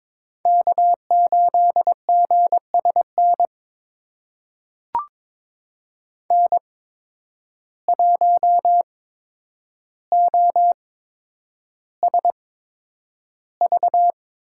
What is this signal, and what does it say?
Morse code, heard as a keyed tone, about 22 wpm, spaced as for 9 wpm, 700 Hz. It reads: K8GSN N1OSV
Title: USA Call Signs - 5 Characters - 8x Character Spacing - 22wpm